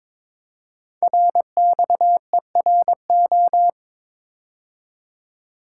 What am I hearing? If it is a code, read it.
Morse code, heard as a keyed tone, about 22 wpm, 700 Hz. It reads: RXERO
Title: Code Groups - 5 Letters - 22wpm